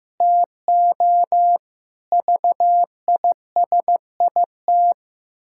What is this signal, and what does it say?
Morse code, heard as a keyed tone, about 15 wpm, 700 Hz. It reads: TO VISIT
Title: Pangram Challenge v2 - 15wpm